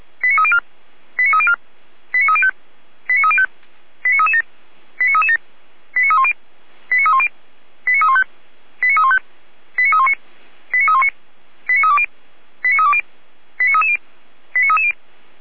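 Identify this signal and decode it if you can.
Five-tone selective call (ZVEI), heard as quick runs of five tones, tones of about 70 ms. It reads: 89304 89304 89305 89305 89337 89337 89310 89310 89315 89315 89319 89319 89320 89320 89330 89330
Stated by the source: ZVEI five-tone selective call received on an SDR.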